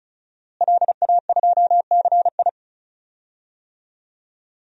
Morse code, heard as a keyed tone, about 36 wpm, 700 Hz. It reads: LA2CI
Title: DXCC-100 Call Signs - 5 Characters - Repeated 2x - 35wpm